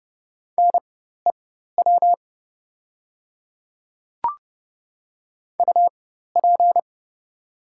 Morse code, heard as a keyed tone, about 30 wpm, spaced as for 14 wpm, 700 Hz. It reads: NEW UP
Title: 4x Character Spacing - Top 500 Words - 30wpm